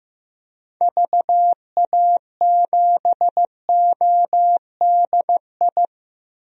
Morse code, heard as a keyed tone, about 15 wpm, 700 Hz. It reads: VA7ODI
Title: DXCC-100 Call Signs - 6 Characters - Repeated 2x - 15wpm